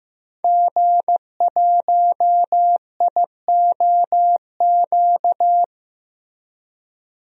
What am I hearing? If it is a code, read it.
Morse code, heard as a keyed tone, about 15 wpm, 700 Hz. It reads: G1IOQ
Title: DXCC-100 Call Signs - 5 Characters - Repeated 3x - 15wpm